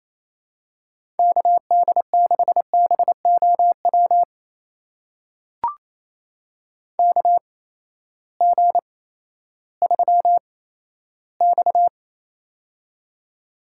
Morse code, heard as a keyed tone, about 28 wpm, spaced as for 14 wpm, 700 Hz. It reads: KD6BOW KG3X
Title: USA Call Signs - 8x Character Spacing - 28wpm